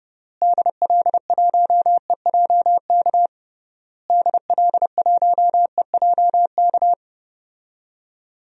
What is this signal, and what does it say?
Morse code, heard as a keyed tone, about 30 wpm, 700 Hz. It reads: DL1EJK DL1EJK
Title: DXCC-100 Call Signs - 6 Characters - Repeated 3x - 30wpm